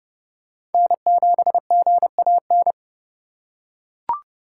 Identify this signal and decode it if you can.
Morse code, heard as a keyed tone, about 30 wpm, 700 Hz. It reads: N7GAN
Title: USA Call Signs - 5 Characters - 4x Character Spacing - 30wpm